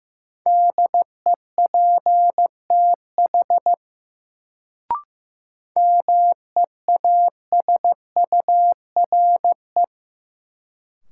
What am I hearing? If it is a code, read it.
Morse code, heard as a keyed tone, about 15 wpm, 700 Hz. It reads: DEPTH MEASURE
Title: Sentences from Top 1200 Words - New Word Drill Alt - 15wpm